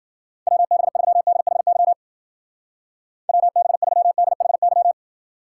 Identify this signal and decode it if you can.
Morse code, heard as a keyed tone, about 49 wpm, 700 Hz. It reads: WB3DHY WB3DHY
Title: Speed Racing - USA Call Signs 40wpm